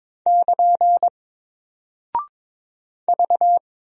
Morse code, heard as a keyed tone, about 21 wpm, 700 Hz. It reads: <KN> V
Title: Single Character (Letter:Number:Punctuation:Prosign) Alt 22wpm